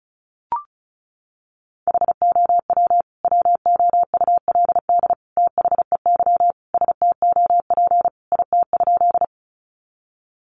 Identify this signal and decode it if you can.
Morse code, heard as a keyed tone, about 34 wpm, 700 Hz. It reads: HOW WOULD THEY STOP IT?
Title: Sentences from Top 300 Words 35wpm